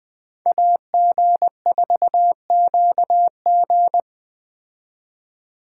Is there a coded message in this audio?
Yes: Morse code from 0.5 to 4.0 s, heard as a keyed tone, about 21 wpm, 700 Hz. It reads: AG4QG